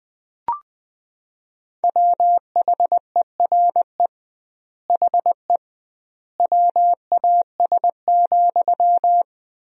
Morse code, heard as a keyed tone, about 20 wpm, 700 Hz. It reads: WHERE HE WAS,
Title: 2x Word-Spacing - Sentences from Top 400 Words - 20wpm